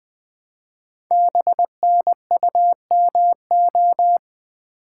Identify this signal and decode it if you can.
Morse code, heard as a keyed tone, about 20 wpm, 700 Hz. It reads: BNUMO
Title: Code Groups - 5 Letters - 20wpm